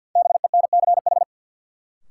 Morse code, heard as a keyed tone, about 50 wpm, 700 Hz. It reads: BENCH
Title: Sentences from Top 1400 Words - New Word Drill Alt - 50wpm